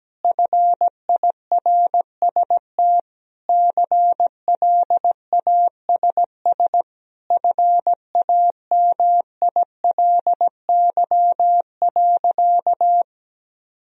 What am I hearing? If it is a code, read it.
Morse code, heard as a keyed tone, about 17 wpm, 700 Hz. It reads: FIRST CLASS FAMILY.
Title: Sentences from Top 1000 Words - Review Part 2 - 17wpm